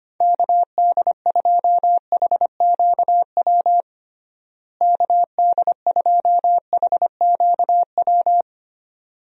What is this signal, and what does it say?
Morse code, heard as a keyed tone, about 25 wpm, 700 Hz. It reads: KD2HQW KD2HQW
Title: USA Call Signs - 6 Characters - Repeated 2x - 25wpm